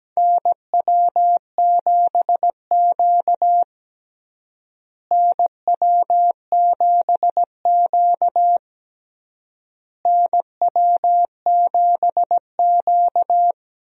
Morse code, heard as a keyed tone, about 17 wpm, 700 Hz. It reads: NW7Q NW7Q NW7Q
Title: USA Call Signs - 4 Characters - Repeated 3x - 17wpm